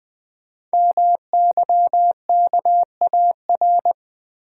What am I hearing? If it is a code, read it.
Morse code, heard as a keyed tone, about 20 wpm, 700 Hz. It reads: MYKAR
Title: Code Groups - 5 Letters - 20wpm